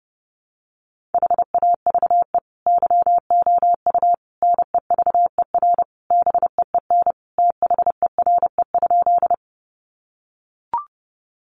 Morse code, heard as a keyed tone, about 30 wpm, 700 Hz. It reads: HAVE YOU NEVER BEEN THERE?
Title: Sentences from Top 300 Words 30wpm